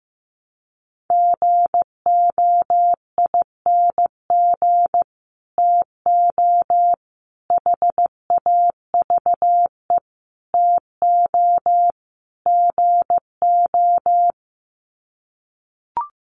Morse code, heard as a keyed tone, about 15 wpm, 700 Hz. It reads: GOING TO HAVE TO GO
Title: Sets of 5 Words Encore 15wpm